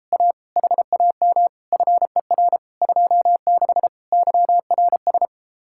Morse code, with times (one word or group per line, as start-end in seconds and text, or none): 0.1-0.3 A
0.6-1.5 HAM
1.7-2.6 FER
2.8-3.9 26
4.1-5.3 YRS